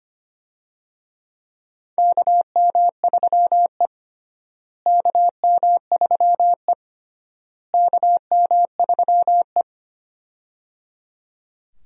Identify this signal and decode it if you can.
Morse code, heard as a keyed tone, about 25 wpm, 700 Hz. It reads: KM3E KM3E KM3E
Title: USA Call Signs - 4 Characters - Repeated 3x - 25wpm